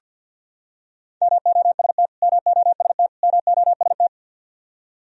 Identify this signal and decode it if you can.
Morse code, heard as a keyed tone, about 51 wpm, 700 Hz. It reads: MOST MOST MOST
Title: Top 100 Words Repeated 50wpm